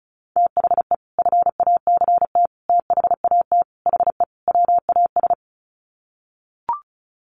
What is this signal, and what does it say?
Morse code, heard as a keyed tone, about 34 wpm, 700 Hz. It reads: THE FACT THAT HE WAS
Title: Sets of 5 Words 35wpm